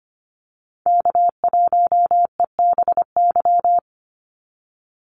Morse code, heard as a keyed tone, about 25 wpm, 700 Hz. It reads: K1EBY
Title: Call Signs Repeated 2x - 25wpm